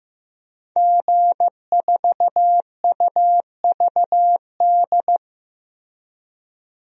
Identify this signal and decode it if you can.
Morse code, heard as a keyed tone, about 15 wpm, 700 Hz. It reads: G4UVD
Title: DXCC-100 Call Signs - 5 Characters - Repeated 2x - 15wpm